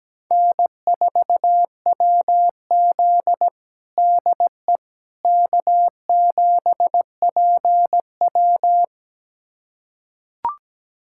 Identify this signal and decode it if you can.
Morse code, heard as a keyed tone, about 17 wpm, 700 Hz. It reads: N4WZ DE K7PW
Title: USA Callsign DE Callsign - 4 Characters - 17wpm